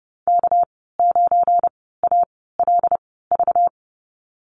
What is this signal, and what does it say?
Morse code, heard as a keyed tone, about 30 wpm, spaced as for 17 wpm, 700 Hz. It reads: K9ALV
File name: Lesson 37g - Letter x - Call Signs 30wpm